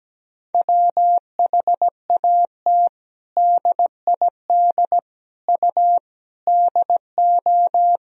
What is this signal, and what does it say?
Morse code, heard as a keyed tone, about 17 wpm, 700 Hz. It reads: WHAT DID U DO
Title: Sentences from Top 100 Words and Abbreviations 17wpm